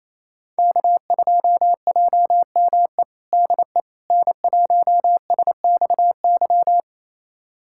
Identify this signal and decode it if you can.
Morse code, heard as a keyed tone, about 28 wpm, 700 Hz. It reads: K2JME DE N1SXY
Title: USA Callsign DE Callsign - 5 Characters - 28wpm